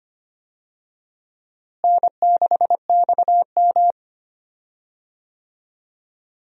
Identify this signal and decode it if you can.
Morse code, heard as a keyed tone, about 25 wpm, 700 Hz. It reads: N6XM
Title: USA Call Signs - 4 Characters - Repeated 2x - 25wpm